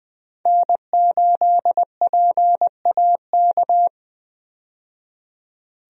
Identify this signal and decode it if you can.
Morse code, heard as a keyed tone, about 20 wpm, 700 Hz. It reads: N8PAK
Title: USA Call Signs - 5 Characters - 20wpm